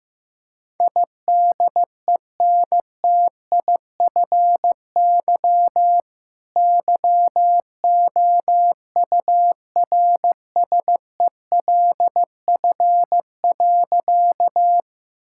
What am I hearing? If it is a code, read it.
Morse code, heard as a keyed tone, about 16 wpm, 700 Hz. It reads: IDENTIFY YOURSELF.
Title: Sentences from Top 1000 Words - Review Part 6 - 15wpm